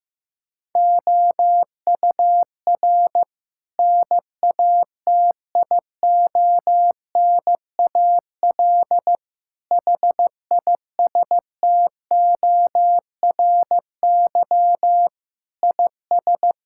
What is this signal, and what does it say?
Morse code, heard as a keyed tone, about 15 wpm, 700 Hz. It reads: OUR NATIONAL HISTORY IS
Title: Sentences from Top 1000 Words - Review Part 6 - 15wpm